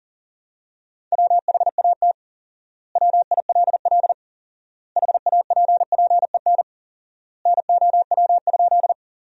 Morse code, heard as a keyed tone, about 40 wpm, 700 Hz. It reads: WHAT WILL HAPPEN NOW?
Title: 4x Word-Spacing - Sentences from Top 400 Words - 40wpm